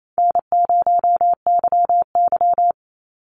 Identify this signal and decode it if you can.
Morse code, heard as a keyed tone, about 28 wpm, 700 Hz. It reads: N0YY
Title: Call Signs Repeated 2x - 28wpm